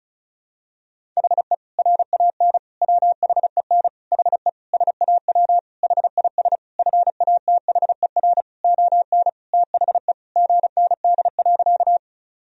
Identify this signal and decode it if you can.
Morse code, heard as a keyed tone, about 36 wpm, 700 Hz. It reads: HE RAN WHEN HE SAW HIS FATHER ON THE GND.
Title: Sentences from Top 500 Words and Abbreviations 35wpm